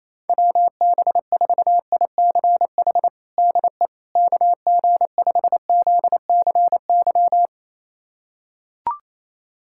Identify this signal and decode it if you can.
Morse code, heard as a keyed tone, about 28 wpm, 700 Hz. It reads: WB4ICH DE KG5ZCY
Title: USA Callsign DE Callsign - 6 Characters - 28wpm